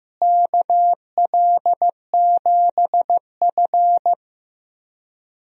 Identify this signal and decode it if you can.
Morse code, heard as a keyed tone, about 15 wpm, 700 Hz. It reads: KL7F
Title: DXCC-100 Call Signs - 4 Characters - Repeated 3x - 15wpm